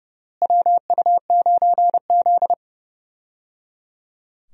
Morse code, heard as a keyed tone, about 30 wpm, 700 Hz. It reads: WU9Z
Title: USA Call Signs - 4 Characters - Repeated 3x - 30wpm